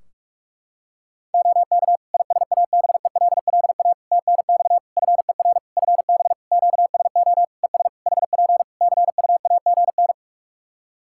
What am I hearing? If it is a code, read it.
Morse code, heard as a keyed tone, about 46 wpm, 700 Hz. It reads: OK ISABELLA TNX FER FB QSO ES HP CUAGN
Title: Practice QSO - Exchange 4 - Ending - 45wpm